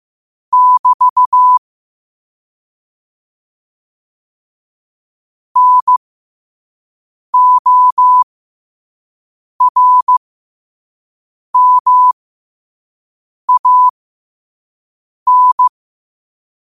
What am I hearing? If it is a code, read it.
Morse code, heard as a keyed tone, about 15 wpm, spaced as for 5 wpm, 1000 Hz. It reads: <BT> NORMAN
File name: qotdfiles / 5wpm